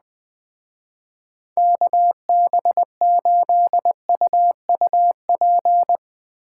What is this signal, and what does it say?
Morse code, heard as a keyed tone, about 20 wpm, 700 Hz. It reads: KB8UUP
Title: Speed Racing - USA 2x3 Call Signs 17wpm